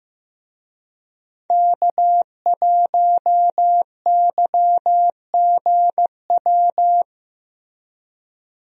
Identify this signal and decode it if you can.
Morse code, heard as a keyed tone, about 15 wpm, 700 Hz. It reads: K1YGW
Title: USA Call Signs - 5 Characters - Repeated 3x - 15wpm